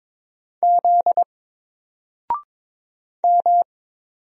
Morse code, heard as a keyed tone, about 22 wpm, 700 Hz. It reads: Z M
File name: Single-Letter-Number-22wpm